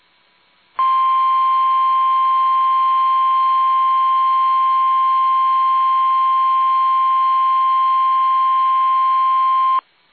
Emergency Alert System (SAME) tones (the 1050 Hz attention tone). It is heard from 0.7 to 9.8 s.
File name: eas / KUSO / KUSO